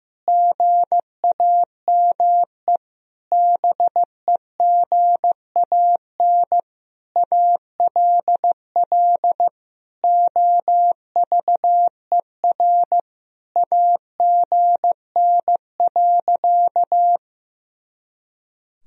Morse code, heard as a keyed tone, about 15 wpm, 700 Hz. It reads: GAME BEGAN ALL OVER AGN.
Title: Sentences from Top 500 Words and Abbreviations 15wpm